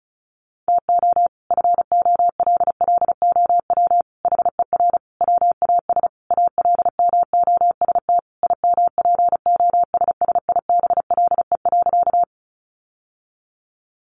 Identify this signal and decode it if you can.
Morse code, heard as a keyed tone, about 34 wpm, 700 Hz. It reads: TO FOLLOW HER WAS ALMOST IMPOSSIBLE.
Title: Sentences from Top 1100 Words 35wpm